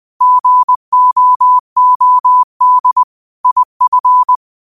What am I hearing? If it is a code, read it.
Morse code, heard as a keyed tone, about 20 wpm, 1000 Hz. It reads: GOOD IF
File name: qotdfiles / 20wpm